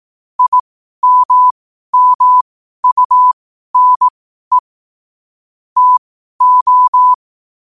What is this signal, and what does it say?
Morse code, heard as a keyed tone, about 18 wpm, spaced as for 12 wpm, 1000 Hz. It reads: IMMUNE TO
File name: qotdfiles / 12wpm